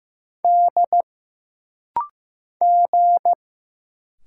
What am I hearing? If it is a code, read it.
Morse code, heard as a keyed tone, about 15 wpm, 700 Hz. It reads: D G
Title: Single Letters - Rapid-Fire 15wpm